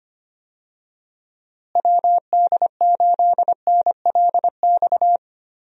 Morse code, heard as a keyed tone, about 25 wpm, 700 Hz. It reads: WD8NLX
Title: USA Call Signs - 6 Characters - Repeated 2x - 25wpm